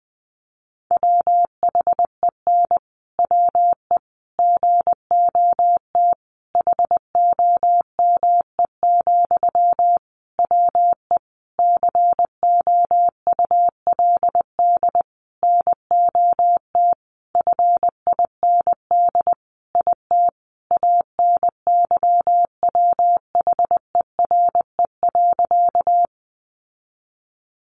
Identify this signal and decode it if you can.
Morse code, heard as a keyed tone, about 20 wpm, 700 Hz. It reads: WHEN WE GOT HOME, WE COULD NOT FIND IT ANYWHERE.
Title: Sentences from Top 1400 Words 20wpm